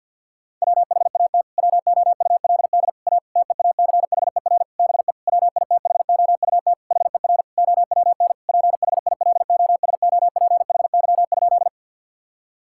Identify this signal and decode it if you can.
Morse code, heard as a keyed tone, about 50 wpm, 700 Hz. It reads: WHAT WOULD A TEACHER BE WITHOUT HER OWN PHILOSOPHY?
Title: Sentences from Top 1000 Words - Review Part 6 - 50wpm